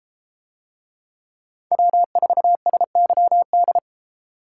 Morse code, heard as a keyed tone, about 33 wpm, 700 Hz. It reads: W4SYD